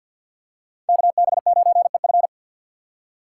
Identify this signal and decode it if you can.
Morse code, heard as a keyed tone, about 53 wpm, 700 Hz. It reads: KB9EF
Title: USA Call Signs - 5 Characters - 6x Character Spacing - 50wpm